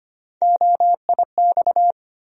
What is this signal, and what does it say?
Morse code, heard as a keyed tone, about 25 wpm, 700 Hz. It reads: OIX
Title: USA Call Sign Postfixes - 3 Letters 25wpm